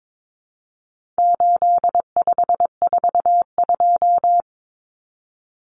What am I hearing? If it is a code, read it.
Morse code, heard as a keyed tone, about 22 wpm, 700 Hz. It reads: 8542
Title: Common Cut Numbers - 4-Digit 22wpm